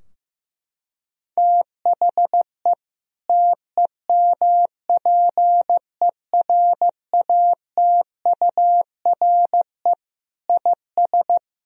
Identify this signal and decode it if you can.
Morse code, heard as a keyed tone, about 15 wpm, 700 Hz. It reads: THE TEMPERATURE IS
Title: Sentences from Top 1000 Words - Review Part 4 - 15wpm